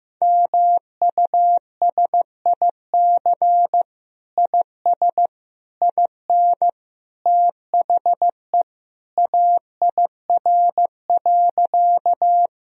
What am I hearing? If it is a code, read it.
Morse code, heard as a keyed tone, about 15 wpm, 700 Hz. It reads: MUSIC IS IN THE AIR.
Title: Sentences from Top 1000 Words - Review Part 1 - 15wpm